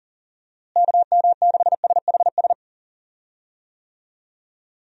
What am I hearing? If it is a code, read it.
Morse code, heard as a keyed tone, about 40 wpm, 700 Hz. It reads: KM6SHS
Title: USA Call Signs - 6 Characters - Repeated 2x - 40wpm